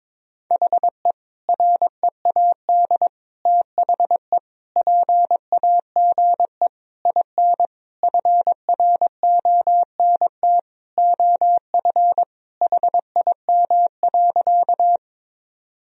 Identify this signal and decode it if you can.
Morse code, heard as a keyed tone, about 21 wpm, 700 Hz. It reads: HE READ THE PAGE IN FRONT OF HIM.
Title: Sentences from Top 1000 Words - Review Part 1 - 22wpm